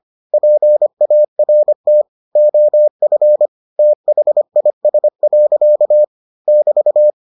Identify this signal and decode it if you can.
Morse code, heard as a keyed tone, about 25 wpm, 600 Hz. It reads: PART OF THIS. <BT>